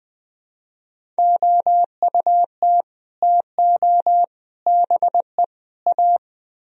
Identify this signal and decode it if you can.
Morse code, heard as a keyed tone, about 20 wpm, 700 Hz. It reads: OUT TO BE A
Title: Speed-Racing - Sets of 4 Words 17wpm